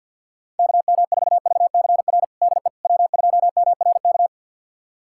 Morse code, heard as a keyed tone, about 49 wpm, 700 Hz. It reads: KM4VCR DE W2MRK
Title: USA Callsign DE Callsign 50wpm